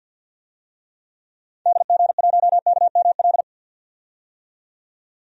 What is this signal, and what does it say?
Morse code, heard as a keyed tone, about 50 wpm, 700 Hz. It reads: DG1KML
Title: DXCC-100 Call Signs - 6 Characters - 50wpm